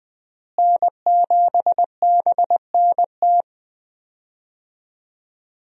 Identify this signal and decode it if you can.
Morse code, heard as a keyed tone, about 20 wpm, 700 Hz. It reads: N7BNT
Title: USA Call Signs - 5 Characters - Repeated 2x - 20wpm